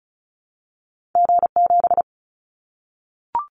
Morse code, heard as a keyed tone, about 34 wpm, 700 Hz. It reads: G7
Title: Callsign Prefixes for Top 25 DXCC - Repeated 3x - 35wpm